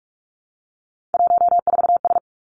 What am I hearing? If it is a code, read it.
Morse code, heard as a keyed tone, about 46 wpm, 700 Hz. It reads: 14S
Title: Three Challenging Letter-Number Combinations 45wpm